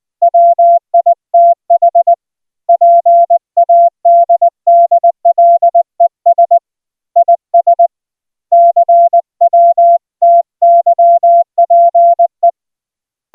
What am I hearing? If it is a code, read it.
Morse code, heard as a keyed tone, about 20 wpm, 690 Hz. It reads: WITH PADDLES IS CWTYPE